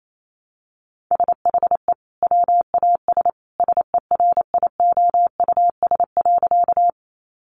Morse code, heard as a keyed tone, about 28 wpm, 700 Hz. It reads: SHE WAS SERIOUS.
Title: Sentences from Top 1100 Words 28wpm